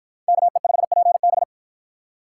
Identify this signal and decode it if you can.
Morse code, heard as a keyed tone, about 50 wpm, 700 Hz. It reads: KE5PB